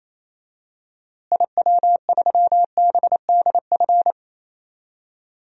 Morse code, heard as a keyed tone, about 28 wpm, 700 Hz. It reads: IW3BDF
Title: DXCC-100 Call Signs - 6 Characters - Repeated 3x - 28wpm